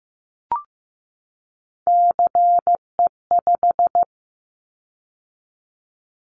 Morse code, heard as a keyed tone, about 15 wpm, 700 Hz. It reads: CE5
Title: Callsign Prefixes for Top 100 DXCC - 15wpm